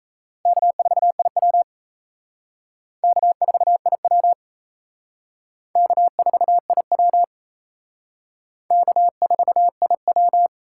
Morse code, heard as a keyed tone, about 34 wpm, 700 Hz. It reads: K4IW K4IW K4IW K4IW
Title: Speed Racing - USA 1x2 Call Signs 28wpm